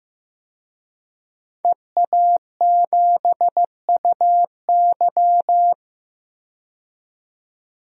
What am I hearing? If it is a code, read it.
Morse code, heard as a keyed tone, about 15 wpm, 700 Hz. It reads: EA7UY
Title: DXCC-100 Call Signs - 5 Characters - 15wpm